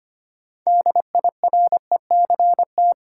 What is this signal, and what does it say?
Morse code, heard as a keyed tone, about 25 wpm, 700 Hz. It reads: DIRECT
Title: Six Letter Words - 25wpm